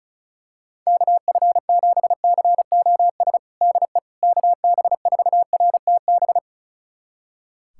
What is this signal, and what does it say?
Morse code, heard as a keyed tone, about 36 wpm, 700 Hz. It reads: KF7COS DE KB4RTB
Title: USA Callsign DE Callsign - 6 Characters - 35wpm